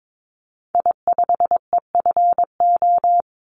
Morse code, heard as a keyed tone, about 22 wpm, 700 Hz. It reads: I5EFO
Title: Call Signs Repeated 3x - 22wpm